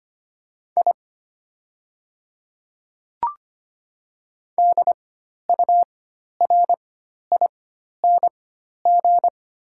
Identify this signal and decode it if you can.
Morse code, heard as a keyed tone, about 25 wpm, spaced as for 12 wpm, 700 Hz. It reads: I DURING
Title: 4x Character Spacing - Top 500 Words - 25wpm